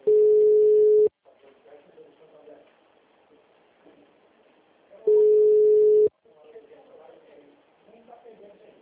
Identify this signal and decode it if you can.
Ringback tone (Europe, 425 Hz), with speech.
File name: out- -2011- - - .11777